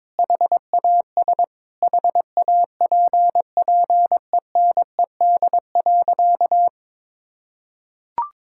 Morse code, heard as a keyed tone, about 22 wpm, 700 Hz. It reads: HAS HAPPENED.
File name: practice / 22wpm / Sentences with 8-Word Phrases - 22wpm